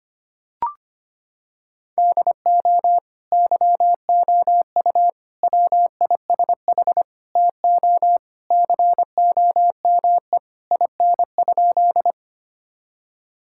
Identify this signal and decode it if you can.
Morse code, heard as a keyed tone, about 25 wpm, 700 Hz. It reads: DO YOU WISH TO COME IN?
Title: Sentences from Top 1600 Words 25wpm